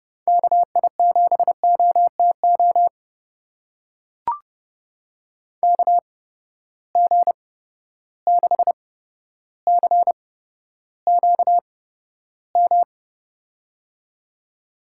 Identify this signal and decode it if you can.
Morse code, heard as a keyed tone, about 32 wpm, spaced as for 13 wpm, 700 Hz. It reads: KI7OTO KG6CQM
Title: USA Call Signs - 6 Characters - 8x Character Spacing - 30wpm